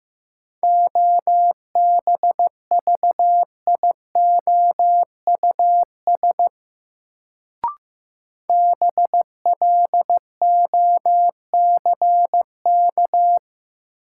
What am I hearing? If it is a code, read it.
Morse code, heard as a keyed tone, about 15 wpm, 700 Hz. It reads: OBVIOUS BLOCK
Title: Sentences from Top 1600 Words - New Word Drill - 15wpm